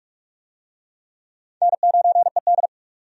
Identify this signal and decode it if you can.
Morse code, heard as a keyed tone, about 46 wpm, 700 Hz. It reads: N9ED
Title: USA Call Signs - 4 Characters - 45wpm